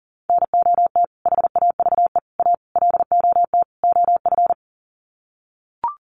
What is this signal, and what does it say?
Morse code, heard as a keyed tone, about 37 wpm, 700 Hz. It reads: NOT HAVE A LOT OF
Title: Sets of 5 Words 40wpm